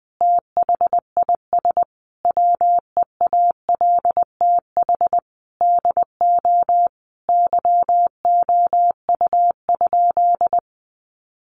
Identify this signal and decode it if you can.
Morse code, heard as a keyed tone, about 20 wpm, 700 Hz. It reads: THIS WEALTH DO YOU?
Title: Sentences from Top 1200 Words 20wpm